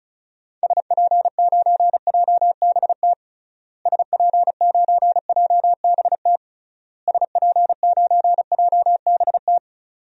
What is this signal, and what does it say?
Morse code, heard as a keyed tone, about 35 wpm, 700 Hz. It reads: SP9JBT SP9JBT SP9JBT
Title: DXCC-100 Call Signs - 6 Characters - Repeated 3x - 35wpm